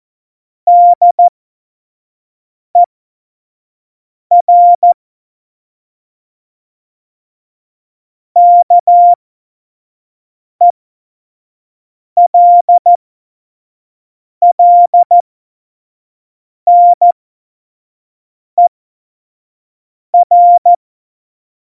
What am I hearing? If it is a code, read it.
Morse code, heard as a keyed tone, about 14 wpm, spaced as for 5 wpm, 700 Hz. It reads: DER KELLNER
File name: Morsekurs / Witze / Witz 03.Dezember 2019, 25 F70